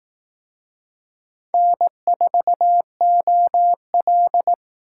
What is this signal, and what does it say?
Morse code, heard as a keyed tone, about 19 wpm, 700 Hz. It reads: N4OL
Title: Speed Racing - USA 1x2 Call Signs 15wpm